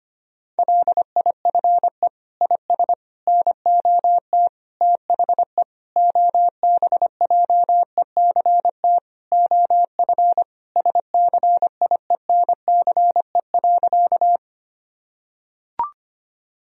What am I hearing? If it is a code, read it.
Morse code, heard as a keyed tone, about 25 wpm, 700 Hz. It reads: LIFE IS NOT THE OBJECT OF SCIENCE.
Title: Sentences from Top 1000 Words - Review Part 3 - 25wpm